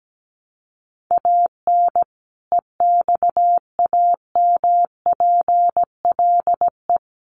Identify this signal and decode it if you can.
Morse code, heard as a keyed tone, about 17 wpm, 700 Hz. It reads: AN EXAMPLE
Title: Sentences from Top 400 Words 17wpm